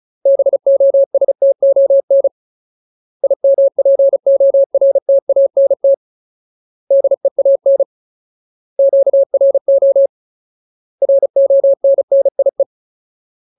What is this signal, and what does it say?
Morse code, heard as a keyed tone, about 34 wpm, 550 Hz. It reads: BOSTON IMPORTANT DEAN QRO RONNIE
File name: Ultimate Word List / 35WPM Ultimate